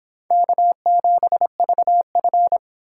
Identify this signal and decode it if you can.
Morse code, heard as a keyed tone, about 31 wpm, 700 Hz. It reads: K7VF